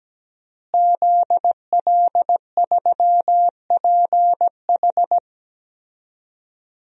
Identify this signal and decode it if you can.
Morse code, heard as a keyed tone, about 17 wpm, 700 Hz. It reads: ZL3PH
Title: DXCC-100 Call Signs - 5 Characters - Repeated 2x - 17wpm